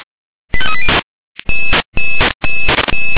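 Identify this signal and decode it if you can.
Five-tone selective call (ZVEI), heard as quick runs of five tones, tones of about 80 ms. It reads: 85380
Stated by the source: ZVEI five-tone selective call received on an SDR.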